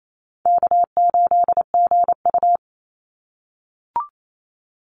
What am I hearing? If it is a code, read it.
Morse code, heard as a keyed tone, about 28 wpm, 700 Hz. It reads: K8GU
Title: Easy Call Signs - 28wpm